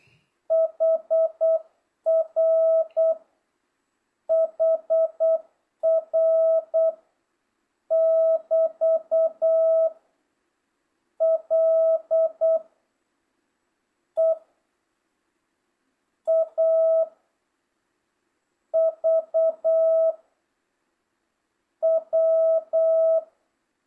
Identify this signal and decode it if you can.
Morse code, heard as a keyed tone, about 8 wpm, 660 Hz. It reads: HR HR <BT> L E A V W